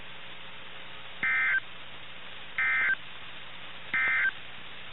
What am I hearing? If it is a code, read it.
Emergency Alert System (SAME) tones, heard as harsh buzzing bursts. It is heard from 1.1 to 4.4 s.
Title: NOAA Weather Radio - KIH61